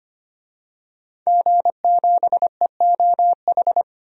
Morse code, heard as a keyed tone, about 25 wpm, 700 Hz. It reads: G7EOH
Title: DXCC-100 Call Signs - 5 Characters - 25wpm